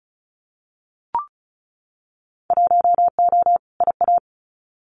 Morse code, heard as a keyed tone, about 36 wpm, 700 Hz. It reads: 1O IA
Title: Winter Field Day Exchanges Encore 35wpm